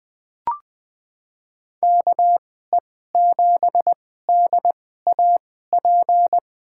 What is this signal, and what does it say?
Morse code, heard as a keyed tone, about 20 wpm, spaced as for 15 wpm, 700 Hz. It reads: KE7DAP
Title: USA Call Signs - 6 Characters - 2x Character Spacing - 20wpm